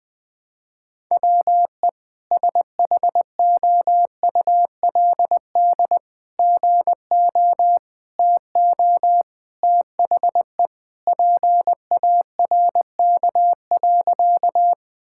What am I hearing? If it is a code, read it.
Morse code, heard as a keyed tone, about 20 wpm, 700 Hz. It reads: WE SHOULD GO TO THE PARK.
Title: Sentences with 3-Word Phrases - 20wpm